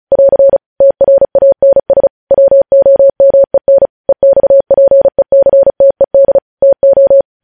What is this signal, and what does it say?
Morse code, heard as a keyed tone, about 36 wpm, 550 Hz. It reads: <AR> TRANS WOMEN EXPECTED TO
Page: news headlines in morse code at 35 wpm